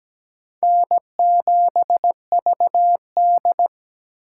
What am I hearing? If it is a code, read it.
Morse code, heard as a keyed tone, about 17 wpm, 700 Hz. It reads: N7VD
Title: USA Call Signs - 4x Character Spacing - 17wpm